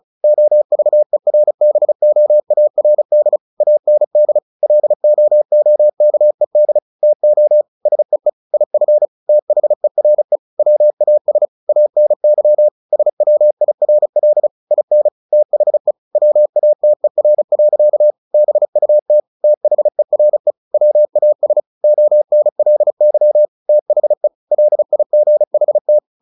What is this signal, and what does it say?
Morse code, heard as a keyed tone, about 34 wpm, 600 Hz. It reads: OVERBOARD AND LOOKED TO SEE IF THERE WAS ANY SWIRL IN THE WATER. BUT THERE WAS ONLY THE LIGHT
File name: the old man and the sea 2 35wpm